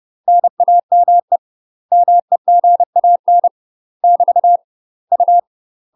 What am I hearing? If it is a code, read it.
Morse code, heard as a keyed tone, about 30 wpm, 700 Hz. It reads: NAME MEGAN <BT> U
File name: UK MASTER QSOs TRAINING / 30WPM / UK MASTER QSOs 0017